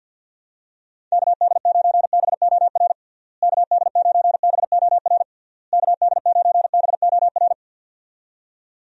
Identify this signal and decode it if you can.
Morse code, heard as a keyed tone, about 50 wpm, 700 Hz. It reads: KD9BOR KD9BOR KD9BOR
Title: USA Call Signs - 6 Characters - Repeated 3x - 50wpm